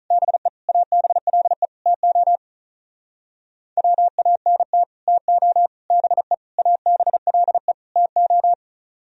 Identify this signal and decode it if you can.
Morse code, heard as a keyed tone, about 45 wpm, 700 Hz. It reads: BE ABLE TO WANT TO BE ABLE TO